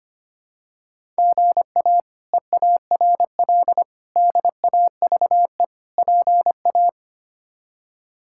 Morse code, heard as a keyed tone, about 25 wpm, 700 Hz. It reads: GA EARL DAVE PA
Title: K1USN SST QSO Practice - 25wpm